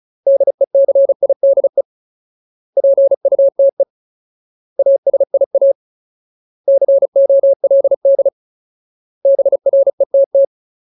Morse code, heard as a keyed tone, about 34 wpm, 550 Hz. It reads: DECIDE PUTE ASIA COLD BRETT
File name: Ultimate Word List / 35WPM Ultimate